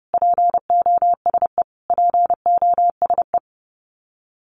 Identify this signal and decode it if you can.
Morse code, heard as a keyed tone, about 30 wpm, 700 Hz. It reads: POSE POSE
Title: Top 500 Words Repeated 30wpm